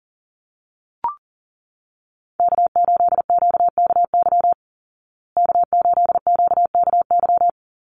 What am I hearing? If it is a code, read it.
Morse code, heard as a keyed tone, about 40 wpm, 700 Hz. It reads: K8QKY K8QKY
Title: Call Signs Repeated 3x - 40wpm